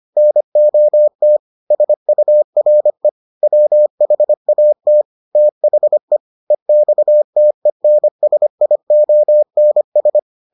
Morse code, heard as a keyed tone, about 25 wpm, 600 Hz. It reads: NOT SURE WHAT THE EXTENSIONS